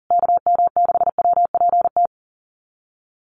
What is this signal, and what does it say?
Morse code, heard as a keyed tone, about 40 wpm, 700 Hz. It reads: KM6WPT
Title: USA Call Signs Repeated 3x 40wpm